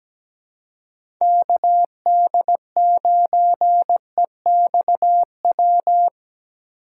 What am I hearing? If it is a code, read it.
Morse code, heard as a keyed tone, about 17 wpm, 700 Hz. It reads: KD9EXW
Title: USA Call Signs - 2x Character Spacing - 17wpm